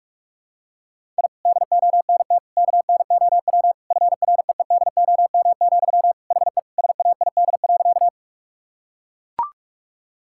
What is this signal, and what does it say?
Morse code, heard as a keyed tone, about 42 wpm, 700 Hz. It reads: I DONT KNOW FREEDOM, HE SAID.
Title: Sentences from Top 1000 Words - Review Part 5 - 45wpm